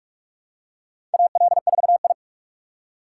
Morse code, heard as a keyed tone, about 43 wpm, 700 Hz. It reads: AL4I
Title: USA Call Signs - 4 Characters - 8x Character Spacing - 45wpm